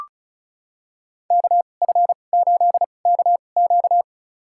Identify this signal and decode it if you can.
Morse code, heard as a keyed tone, about 36 wpm, spaced as for 25 wpm, 700 Hz. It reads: KF8KQ
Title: USA Call Signs - 5 Characters - 2x Character Spacing - 35wpm